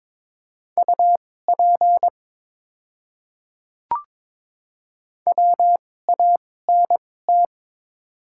Morse code, heard as a keyed tone, about 23 wpm, spaced as for 16 wpm, 700 Hz. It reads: UP WANT
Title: 2x Character Spacing - Top 200 Words - 22wpm